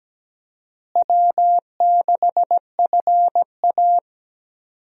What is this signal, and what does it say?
Morse code, heard as a keyed tone, about 17 wpm, 700 Hz. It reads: W6FA
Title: USA Call Signs - 4 Characters - Repeated 3x - 17wpm